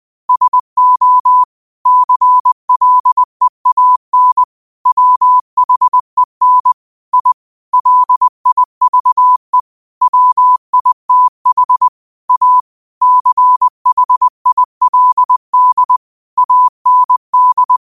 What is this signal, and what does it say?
Morse code, heard as a keyed tone, about 20 wpm, 1000 Hz. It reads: SO CLEAN WHEN I LIVE WITH A CHILD AND
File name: qotdfiles / 20wpm